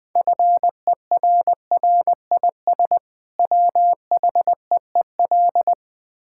Morse code, heard as a keyed tone, about 20 wpm, 700 Hz. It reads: FERRIS WHEEL
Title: Easy Words with Hints - 20wpm